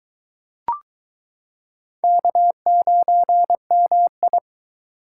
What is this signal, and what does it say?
Morse code, heard as a keyed tone, about 22 wpm, 700 Hz. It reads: K9MI